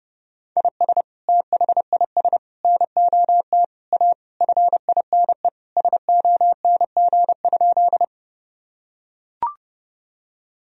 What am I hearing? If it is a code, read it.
Morse code, heard as a keyed tone, about 32 wpm, 700 Hz. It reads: IS THIS NOT A FINE SONG?
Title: Sentences from Top 1000 Words - Review Part 2 - 30wpm